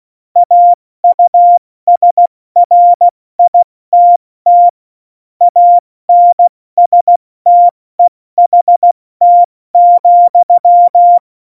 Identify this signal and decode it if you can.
Morse code, heard as a keyed tone, about 16 wpm, spaced as for 14 wpm, 700 Hz. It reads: AUSRITT ANSTEHT,
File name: Morsekurs / Witze / Witz 03.Dezember 2019, 70 F80